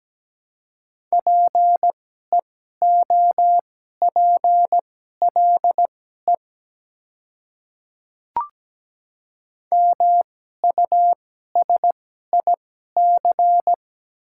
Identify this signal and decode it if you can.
Morse code, heard as a keyed tone, about 17 wpm, spaced as for 12 wpm, 700 Hz. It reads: PEOPLE MUSIC
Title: 2x Character Spacing - Top 300 Words - 17wpm